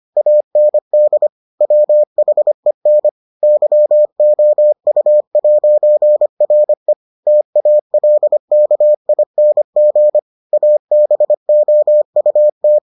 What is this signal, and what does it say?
Morse code, heard as a keyed tone, about 25 wpm, 600 Hz. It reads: AND WHEN YOU'RE TALKING ABOUT